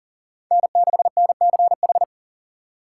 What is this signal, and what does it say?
Morse code, heard as a keyed tone, about 40 wpm, 700 Hz. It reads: N6NCH
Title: USA Call Signs - 5 Characters - 3x Character Spacing - 40wpm